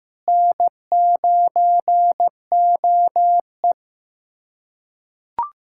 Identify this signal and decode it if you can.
Morse code, heard as a keyed tone, about 15 wpm, 700 Hz. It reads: N9OE
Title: USA Callsign DE Callsign - 4 Characters - 15wpm